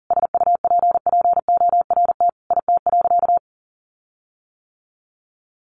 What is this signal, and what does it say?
Morse code, heard as a keyed tone, about 40 wpm, 700 Hz. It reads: SUPPORT IT.
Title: Sentences from Top 1300 Words 40wpm